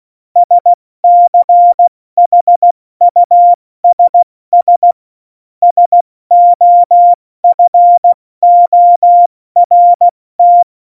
Morse code, heard as a keyed tone, about 16 wpm, 700 Hz. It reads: SCHUSS SOFORT